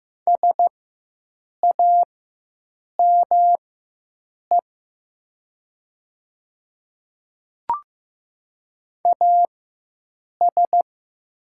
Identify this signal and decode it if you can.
Morse code, heard as a keyed tone, about 15 wpm, spaced as for 7 wpm, 700 Hz. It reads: SAME AS
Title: 4x Character Spacing - Top 300 Words - 15wpm